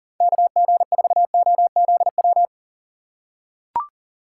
Morse code, heard as a keyed tone, about 40 wpm, 700 Hz. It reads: KG4OZW
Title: USA Call Signs - 6x Character Spacing - 40wpm